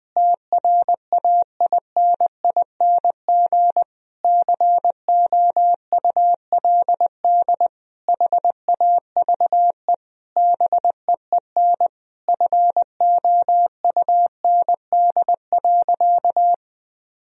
Morse code, heard as a keyed tone, about 20 wpm, 700 Hz. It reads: TRAINING COULD HAVE BEEN FOUND.
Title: Sentences from Top 1000 Words - Review Part 4 - 20wpm